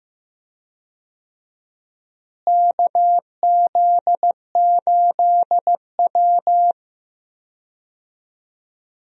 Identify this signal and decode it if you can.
Morse code, heard as a keyed tone, about 15 wpm, 700 Hz. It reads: KZ8W
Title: USA Call Signs - 4 Characters - Repeated 3x - 15wpm